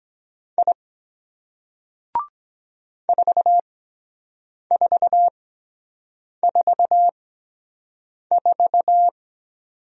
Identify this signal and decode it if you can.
Morse code, heard as a keyed tone, about 23 wpm, spaced as for 7 wpm, 700 Hz. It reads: I 4444